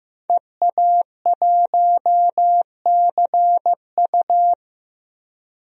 Morse code, heard as a keyed tone, about 15 wpm, 700 Hz. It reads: EA1CU
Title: DXCC-100 Call Signs - 5 Characters - 15wpm